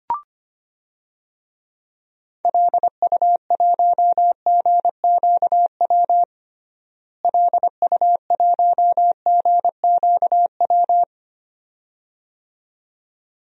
Morse code, heard as a keyed tone, about 25 wpm, 700 Hz. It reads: LU1GQW LU1GQW
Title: DXCC-100 Call Signs - 6 Characters - Repeated 2x - 25wpm